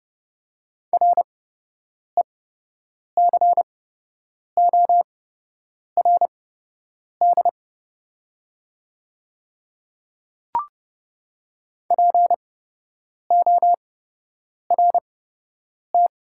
Morse code, heard as a keyed tone, about 30 wpm, spaced as for 8 wpm, 700 Hz. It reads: RECORD PORT
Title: 8x Character Spacing - Top 500 Words - 30wpm